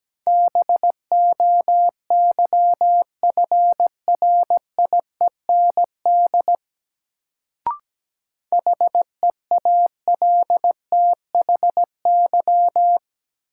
Morse code, heard as a keyed tone, about 17 wpm, 700 Hz. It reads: BOYFRIEND HEALTHY
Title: Sentences from Top 1000 Words - Review Part 5 - New Word Drill - 17wpm